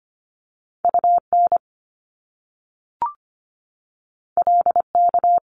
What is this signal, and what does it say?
Morse code, heard as a keyed tone, about 25 wpm, 700 Hz. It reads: UN LK
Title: Top Two Letter Combinations in Words 25wpm